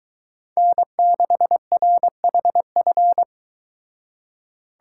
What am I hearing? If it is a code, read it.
Morse code, heard as a keyed tone, about 23 wpm, 700 Hz. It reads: N6RHF